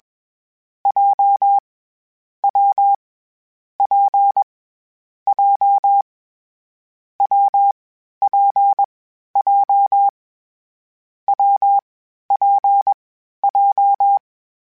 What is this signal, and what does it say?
Morse code, heard as a keyed tone, about 21 wpm, 800 Hz. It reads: J W P J W P J W P J
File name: Variable Speed / Variable Speed-mp3s / WPJ WPJ1 sp @20